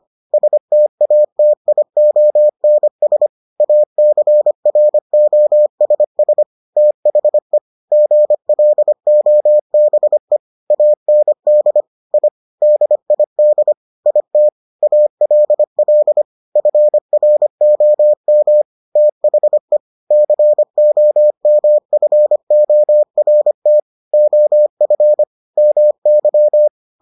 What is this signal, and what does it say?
Morse code, heard as a keyed tone, about 25 wpm, 600 Hz. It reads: STATIONS ACROSS THE GLOBE AND I DID IT ALL FROM THE COMFORT OF MY